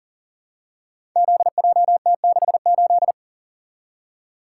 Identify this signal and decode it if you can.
Morse code, heard as a keyed tone, about 39 wpm, 700 Hz. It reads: ZJT68
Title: Code Groups - 5 Letter:Numbers - 40wpm